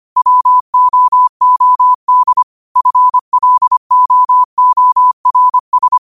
Morse code, heard as a keyed tone, about 25 wpm, 1000 Hz. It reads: WOOD FLOORS